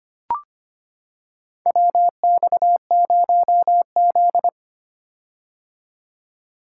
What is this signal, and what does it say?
Morse code, heard as a keyed tone, about 25 wpm, 700 Hz. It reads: WX0Z
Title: K1USN SST QSO Practice - 25wpm